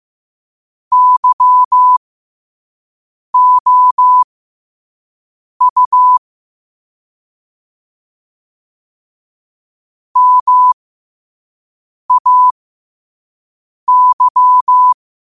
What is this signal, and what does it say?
Morse code, heard as a keyed tone, about 15 wpm, spaced as for 5 wpm, 1000 Hz. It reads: YOU MAY
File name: qotdfiles / 5wpm